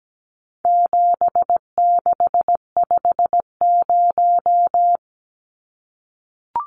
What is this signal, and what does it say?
Morse code, heard as a keyed tone, about 17 wpm, 700 Hz. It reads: 7650
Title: Numbers - 4-Digit 17wpm